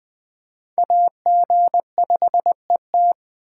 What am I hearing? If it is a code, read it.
Morse code, heard as a keyed tone, about 21 wpm, 700 Hz. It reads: AG5ET